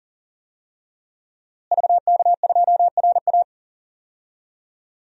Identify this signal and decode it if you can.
Morse code, heard as a keyed tone, about 40 wpm, 700 Hz. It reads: VK2RA
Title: DXCC-100 Call Signs - 5 Characters - Repeated 3x - 40wpm